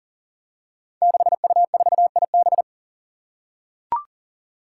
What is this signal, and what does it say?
Morse code, heard as a keyed tone, about 40 wpm, 700 Hz. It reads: 6U4IB
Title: Five Challenging Letter-Number Combinations 40wpm